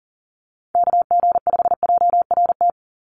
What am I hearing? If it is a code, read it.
Morse code, heard as a keyed tone, about 40 wpm, 700 Hz. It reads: KG5JRT
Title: USA Call Signs 40wpm